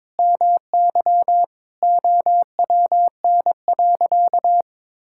Morse code, heard as a keyed tone, about 22 wpm, 700 Hz. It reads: MY OWN.
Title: Sentences from Top 1000 Words - Review Part 6 - 22wpm